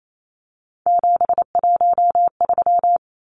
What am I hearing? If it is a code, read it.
Morse code, heard as a keyed tone, about 28 wpm, 700 Hz. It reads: 713
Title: Numbers - 3-Digit 28wpm